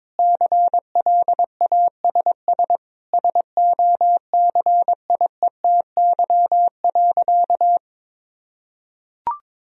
Morse code, heard as a keyed tone, about 21 wpm, 700 Hz. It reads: CLASS SOCIETY.
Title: Sentences from Top 1000 Words - Review Part 4 - 22wpm